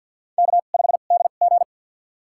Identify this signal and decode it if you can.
Morse code, heard as a keyed tone, about 53 wpm, spaced as for 37 wpm, 700 Hz. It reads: K5DG
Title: USA Call Signs - 2x Character Spacing - 50wpm